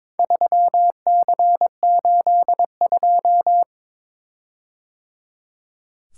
Morse code, heard as a keyed tone, about 22 wpm, 700 Hz. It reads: 3C82
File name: practice / 22wpm / Four Letter-Number Combinations - 22wpm